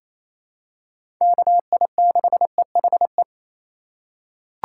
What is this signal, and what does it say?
Morse code, heard as a keyed tone, about 28 wpm, 700 Hz. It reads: KI6EHE
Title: USA Call Signs - 6 Characters - 3x Character Spacing - 28wpm